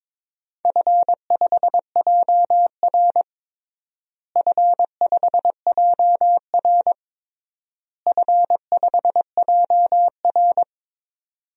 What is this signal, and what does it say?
Morse code, heard as a keyed tone, about 22 wpm, 700 Hz. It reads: F5JR F5JR F5JR
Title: DXCC-100 Call Signs - 4 Characters - Repeated 3x - 22wpm